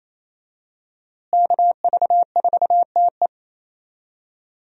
Morse code, heard as a keyed tone, about 28 wpm, 700 Hz. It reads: KV4TE
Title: USA Call Signs - 5 Characters - Repeated 2x - 28wpm